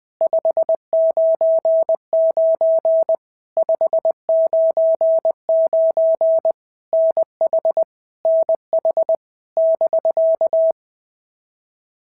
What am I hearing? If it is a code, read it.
Morse code, heard as a keyed tone, about 20 wpm, 650 Hz. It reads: 599 599 NH NH <BK>
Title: POTA QSO Practice - 20wpm